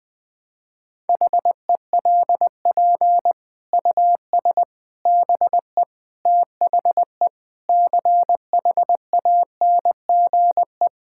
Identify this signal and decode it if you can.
Morse code, heard as a keyed tone, about 20 wpm, 700 Hz. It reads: HELP US BE THE CHANGE
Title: Sentences with 9-Word Phrases - 20wpm